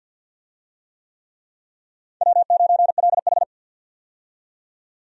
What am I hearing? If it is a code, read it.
Morse code, heard as a keyed tone, about 47 wpm, 700 Hz. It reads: W9LH
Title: USA Call Signs - 4 Characters - 50wpm